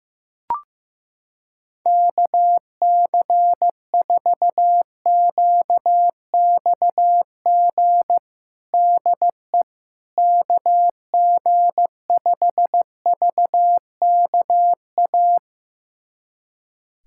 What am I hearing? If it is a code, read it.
Morse code, heard as a keyed tone, about 16 wpm, 700 Hz. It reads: KC4QXG DE KG5VKA
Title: USA Callsign DE Callsign - 6 Characters - 15wpm